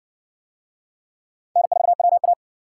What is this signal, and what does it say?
Morse code, heard as a keyed tone, about 53 wpm, 700 Hz. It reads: N4WA